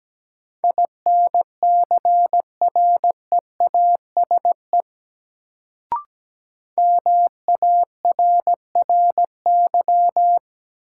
Morse code, heard as a keyed tone, about 17 wpm, 700 Hz. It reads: INCREASE MARRY
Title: Sentences from Top 1200 Words - New Word Drill Alt - 17wpm